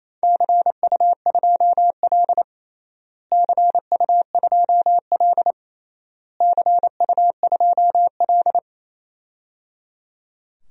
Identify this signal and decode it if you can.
Morse code, heard as a keyed tone, about 28 wpm, 700 Hz. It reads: CU2L CU2L CU2L
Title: DXCC-100 Call Signs - 4 Characters - Repeated 3x - 28wpm